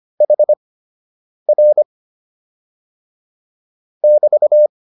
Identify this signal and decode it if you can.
Morse code, heard as a keyed tone, about 25 wpm, spaced as for 8 wpm, 600 Hz. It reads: HR <BT>